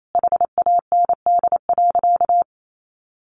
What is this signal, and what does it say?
Morse code, heard as a keyed tone, about 28 wpm, 700 Hz. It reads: HAND.
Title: Sentences from Top 1400 Words 28wpm